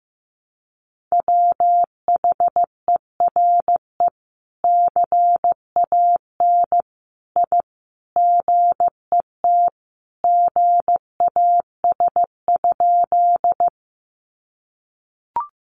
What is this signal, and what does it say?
Morse code, heard as a keyed tone, about 15 wpm, 700 Hz. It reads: WHERE CAN I GET GAS?
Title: Sentences from Top 1100 Words 15wpm